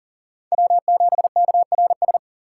Morse code, heard as a keyed tone, about 40 wpm, 700 Hz. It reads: W7KRS
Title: K1USN SST QSO Practice - 40wpm